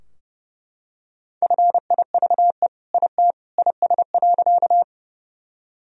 Morse code, heard as a keyed tone, about 30 wpm, 700 Hz. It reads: FIVE IT IS.
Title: Sentences from Top 1000 Words - Review Part 2 - 30wpm